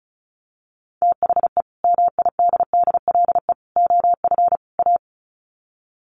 Morse code, heard as a keyed tone, about 34 wpm, 700 Hz. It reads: THE MIDDLE OF A
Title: Sets of 4 Words 35wpm